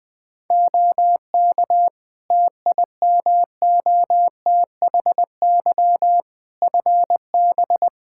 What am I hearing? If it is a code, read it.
Morse code, heard as a keyed tone, about 20 wpm, 700 Hz. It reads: OK TIMOTHY FB
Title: Practice QSO - Exchange 2 - Info - 20wpm